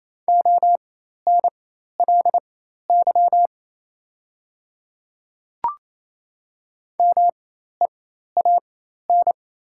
Morse code, heard as a keyed tone, about 28 wpm, spaced as for 13 wpm, 700 Hz. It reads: ONLY MEAN
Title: 4x Character Spacing - Top 500 Words - 28wpm